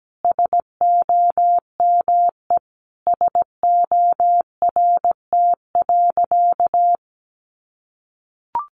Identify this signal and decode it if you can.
Morse code, heard as a keyed tone, about 17 wpm, 700 Hz. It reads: SOME SORT.
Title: Sentences from Top 1300 Words 17wpm